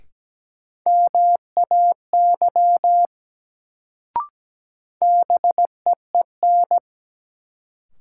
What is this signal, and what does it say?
Morse code, heard as a keyed tone, about 17 wpm, 700 Hz. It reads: MAY BEEN
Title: Sentences from Top 100 Words - New Word Drill - 17wpm